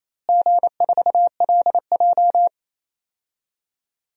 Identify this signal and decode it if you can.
Morse code, heard as a keyed tone, about 28 wpm, 700 Hz. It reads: G4LJ
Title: DXCC-100 Call Signs - 4 Characters - Repeated 3x - 28wpm